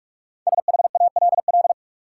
Morse code, heard as a keyed tone, about 47 wpm, 700 Hz. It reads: SHALL
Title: Five Letter Words - 45wpm